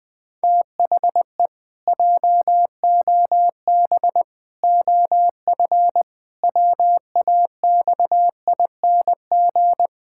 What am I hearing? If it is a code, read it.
Morse code, heard as a keyed tone, about 20 wpm, 700 Hz. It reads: THE JOB OF WAXING
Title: Pangram Challenge v2 - 20wpm